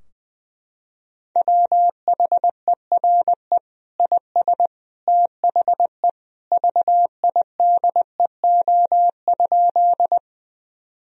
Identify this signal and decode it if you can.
Morse code, heard as a keyed tone, about 20 wpm, 700 Hz. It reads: WHERE IS THE VIDEO?
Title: Sentences from Top 1000 Words - Review Part 4 - 20wpm